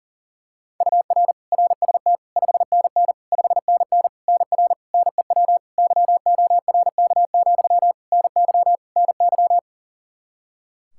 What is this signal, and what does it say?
Morse code, heard as a keyed tone, about 37 wpm, 700 Hz. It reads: UR RST 5NN 5NN NR NEW YORK, NY NY
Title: K1USN SST QSO Practice 40wpm